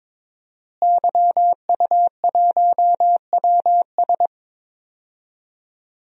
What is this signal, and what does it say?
Morse code, heard as a keyed tone, about 22 wpm, 700 Hz. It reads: YU1WS
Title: DXCC-100 Call Signs - 5 Characters - Repeated 2x - 22wpm